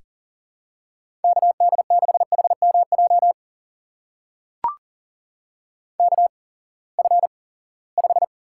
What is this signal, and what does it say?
Morse code, heard as a keyed tone, about 40 wpm, spaced as for 23 wpm, 700 Hz. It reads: KD6HMJ KF5
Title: USA Call Signs - 6 Characters - 8x Character Spacing - 40wpm